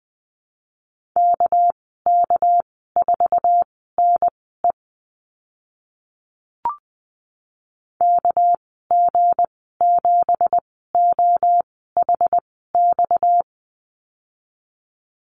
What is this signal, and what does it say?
Morse code, heard as a keyed tone, about 20 wpm, spaced as for 15 wpm, 700 Hz. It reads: KK4NE KG7OHX
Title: 231 Lesson 37g - Letter x - Call Signs 20wpm